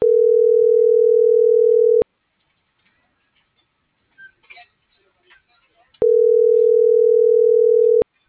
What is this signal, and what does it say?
Ringback tone (North America), 440 + 480 Hz.